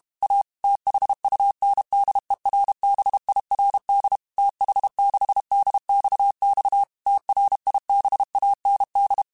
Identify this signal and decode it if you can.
Morse code, heard as a keyed tone, about 32 wpm, 800 Hz. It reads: A THUNDERBIRD TH6DXX TRIBAND
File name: QSOs / QSOs-mp3s / WD0BGP 1@30